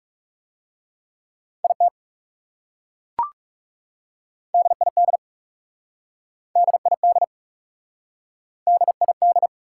Morse code, heard as a keyed tone, about 37 wpm, 700 Hz. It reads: IT DID DID DID
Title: Speed-Racing Top 180 QSO Elements 30wpm